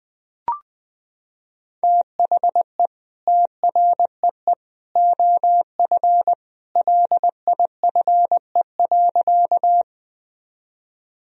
Morse code, heard as a keyed tone, about 20 wpm, 700 Hz. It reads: THE TREE OF LIFE.
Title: Sentences from Top 1000 Words - Review Part 1 - 20wpm